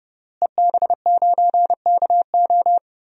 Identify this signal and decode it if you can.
Morse code, heard as a keyed tone, about 30 wpm, 700 Hz. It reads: EB9KO
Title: DXCC-100 Call Signs - 5 Characters - Repeated 3x - 30wpm